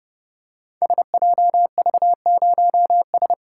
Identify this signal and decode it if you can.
Morse code, heard as a keyed tone, about 29 wpm, 700 Hz. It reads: SJV0S
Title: Code Groups - 5 Letter:Numbers - 30wpm